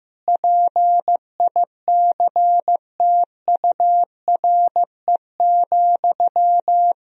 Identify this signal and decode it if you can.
Morse code, heard as a keyed tone, about 16 wpm, 700 Hz. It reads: PICTURE,
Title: Sentences from Top 1000 Words - Review Part 3 - 15wpm